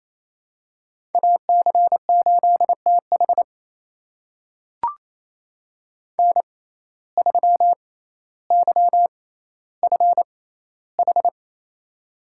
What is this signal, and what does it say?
Morse code, heard as a keyed tone, about 28 wpm, spaced as for 14 wpm, 700 Hz. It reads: AC8TH N3YFH
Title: USA Call Signs - 6x Character Spacing - 28wpm